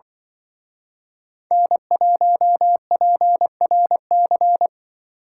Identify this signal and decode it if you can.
Morse code, heard as a keyed tone, about 24 wpm, 700 Hz. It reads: N1PRC
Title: Speed Racing - USA Call Signs 20wpm